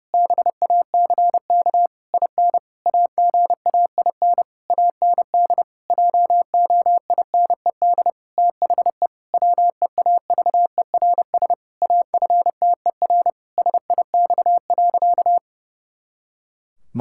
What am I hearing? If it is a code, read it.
Morse code, heard as a keyed tone, about 30 wpm, 700 Hz. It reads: BACK IN AGAIN AND JOINED THE WEAVERS AFTER SIX.
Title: Pangram Challenge v2 - 30wpm